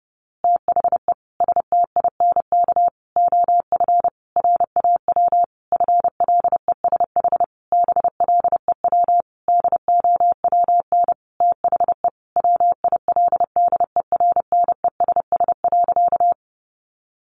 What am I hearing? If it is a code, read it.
Morse code, heard as a keyed tone, about 30 wpm, 700 Hz. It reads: THE STINK OF RAW FLESH BLEW DOWN THE WILDERNESS.
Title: A Sound Of Thunder 30wpm